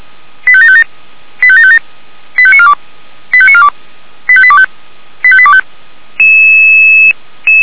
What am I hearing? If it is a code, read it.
Five-tone selective call (ZVEI), heard as quick runs of five tones, tones of about 70 ms. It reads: 85757 85757 85942 85942 85825 85825